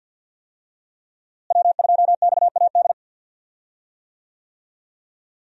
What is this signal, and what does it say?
Morse code, heard as a keyed tone, about 50 wpm, 700 Hz. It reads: W2XAD
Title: USA Call Signs - 5 Characters - Repeated 3x - 50wpm